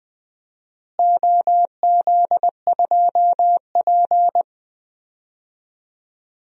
Morse code, heard as a keyed tone, about 21 wpm, 700 Hz. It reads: OZ2P
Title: DXCC-100 Call Signs - 4 Characters - Repeated 3x - 20wpm